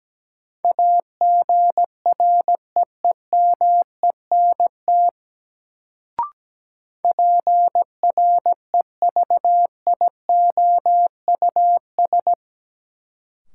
Morse code, heard as a keyed tone, about 17 wpm, 700 Hz. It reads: AGREEMENT PREVIOUS
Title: Sentences from Top 1500 Words - New Word Drill - 17wpm